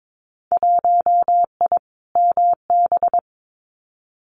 Morse code, heard as a keyed tone, about 22 wpm, 700 Hz. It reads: 1I MB
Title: Winter Field Day Exchanges Encore 22wpm